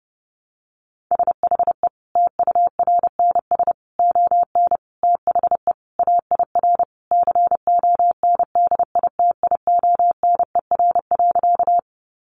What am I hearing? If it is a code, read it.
Morse code, heard as a keyed tone, about 30 wpm, 700 Hz. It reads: SHE TURNS ON THE AIR CONDITIONER.
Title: Easy Phrases 30wpm